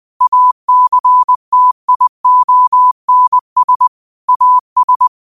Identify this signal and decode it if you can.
Morse code, heard as a keyed tone, about 20 wpm, 1000 Hz. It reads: ACTIONS AS
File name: qotdfiles / 20wpm